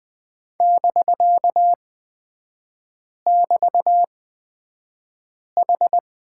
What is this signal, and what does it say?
Morse code, heard as a keyed tone, about 20 wpm, spaced as for 5 wpm, 700 Hz. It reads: <BK><BT>H
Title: 250 Lesson 40e - BK - Three Characters 20wpm